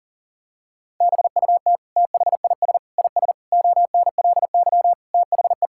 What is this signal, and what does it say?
Morse code, heard as a keyed tone, about 40 wpm, 700 Hz. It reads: BUT THIS IS ONLY THE
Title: Sentences from Top 1000 Words - Review Part 5 - 40wpm